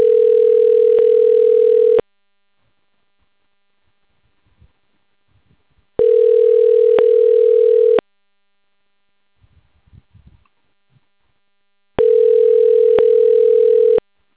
Ringback tone (North America), 440 + 480 Hz.
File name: recordings / RE3a409a09ce1cd55fced515aa5cad5cf9